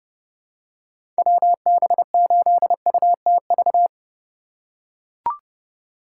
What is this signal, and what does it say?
Morse code, heard as a keyed tone, about 31 wpm, 700 Hz. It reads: WB8UTV
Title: USA Call Signs - 6 Characters - 3x Character Spacing - 30wpm